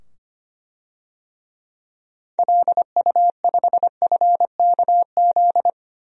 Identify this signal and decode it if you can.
Morse code, heard as a keyed tone, about 25 wpm, 700 Hz. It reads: LU5FKZ
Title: DXCC-100 Call Signs - 6 Characters - 25wpm